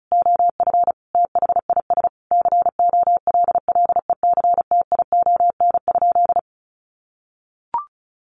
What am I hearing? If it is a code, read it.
Morse code, heard as a keyed tone, about 34 wpm, 700 Hz. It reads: OF THIS COLLECTION?
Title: Sentences from Top 1200 Words 35wpm